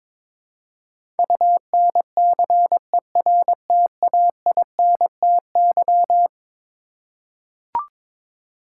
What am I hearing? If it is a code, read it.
Morse code, heard as a keyed tone, about 22 wpm, 700 Hz. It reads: UNCERTAINTY
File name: practice / 22wpm / Eleven Letter Words - 22wpm